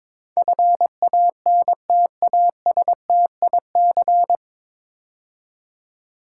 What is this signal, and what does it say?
Morse code, heard as a keyed tone, about 22 wpm, 700 Hz. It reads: FANTASTIC
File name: 9 Letter Words - First 3 Letters Given - 22wpm